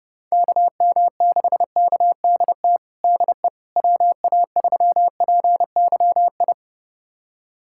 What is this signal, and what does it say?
Morse code, heard as a keyed tone, about 30 wpm, 700 Hz. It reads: KM6KDT DE WA3PYI
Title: USA Callsign DE Callsign - 6 Characters - 30wpm